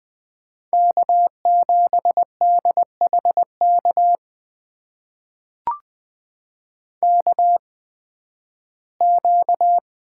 Morse code, heard as a keyed tone, about 20 wpm, spaced as for 13 wpm, 700 Hz. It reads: K7DHK KQ
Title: USA Call Signs - 5 Characters - 8x Character Spacing - 20wpm